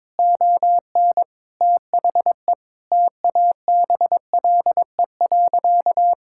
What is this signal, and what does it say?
Morse code, heard as a keyed tone, about 22 wpm, 700 Hz. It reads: ON THE TABLE.
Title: Sentences from Top 1000 Words - Review Part 5 - 22wpm